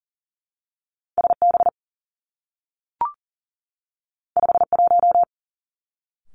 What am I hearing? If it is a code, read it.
Morse code, heard as a keyed tone, about 42 wpm, 700 Hz. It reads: S B 5 1
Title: Two Challenging Letter-Number Combinations 40wpm